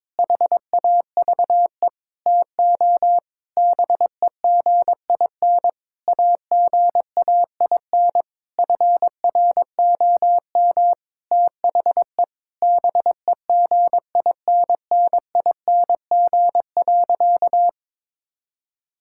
Morse code, heard as a keyed tone, about 22 wpm, 700 Hz. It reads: HAVE TO BEGIN AGAIN FROM THE BEGINNING.
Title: Sentences from Top 1000 Words - Review Part 5 - 22wpm